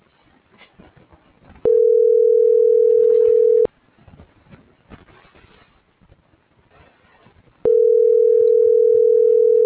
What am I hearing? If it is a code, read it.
Ringback tone (North America), 440 + 480 Hz.
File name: CENTRAL GENERAL / out tania.lemus